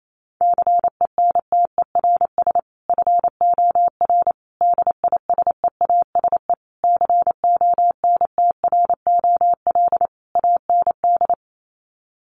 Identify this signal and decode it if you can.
Morse code, heard as a keyed tone, about 28 wpm, 700 Hz. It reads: CENTERS FOR DISEASE CONTROL AND
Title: Sets of 5 Words Encore2 28wpm